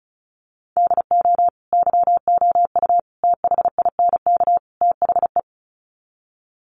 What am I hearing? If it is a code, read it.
Morse code, heard as a keyed tone, about 34 wpm, 700 Hz. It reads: DO YOU THINK THE
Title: Sets of 4 Words 35wpm